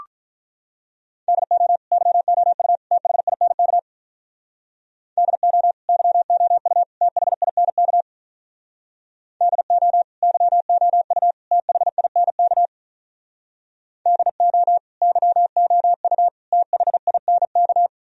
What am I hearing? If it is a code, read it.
Morse code, heard as a keyed tone, about 43 wpm, 700 Hz. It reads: DO YOU THINK DO YOU THINK DO YOU THINK DO YOU THINK
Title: Speed-Racing - Sets of 3 Words 35wpm